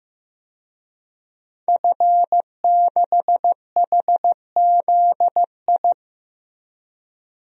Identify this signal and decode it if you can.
Morse code, heard as a keyed tone, about 16 wpm, 700 Hz. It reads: F6HZI
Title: DXCC-100 Call Signs - 5 Characters - Repeated 3x - 15wpm